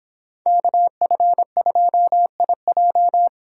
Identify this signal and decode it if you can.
Morse code, heard as a keyed tone, about 25 wpm, 700 Hz. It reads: KF2IJ